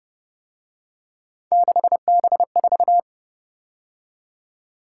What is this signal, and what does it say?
Morse code, heard as a keyed tone, about 30 wpm, 700 Hz. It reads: 6B4